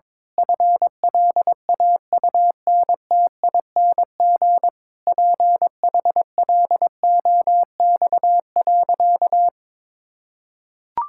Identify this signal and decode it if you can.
Morse code, heard as a keyed tone, about 22 wpm, 700 Hz. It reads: FLAUNTING PHLOX.
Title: Pangram Challenge v2 - 22wpm